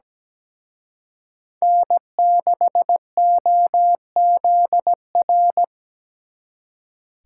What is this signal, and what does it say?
Morse code, heard as a keyed tone, about 20 wpm, 700 Hz. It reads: N6OZR
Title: Speed Racing - USA 1x3 Call Signs 17wpm